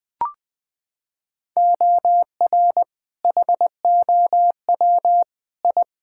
Morse code, heard as a keyed tone, about 20 wpm, 700 Hz. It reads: OR HOW I
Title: Sets of 8 Words - 20wpm